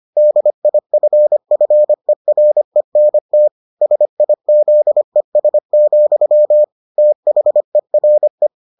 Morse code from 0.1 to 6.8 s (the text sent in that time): DIFFERENT SIZES,